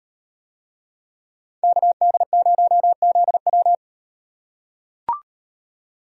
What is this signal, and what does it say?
Morse code, heard as a keyed tone, about 38 wpm, 700 Hz. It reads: KD0ZW